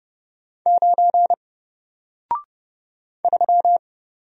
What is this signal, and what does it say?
Morse code, heard as a keyed tone, about 30 wpm, 700 Hz. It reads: 9 3
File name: Single-Letter-Number-30wpm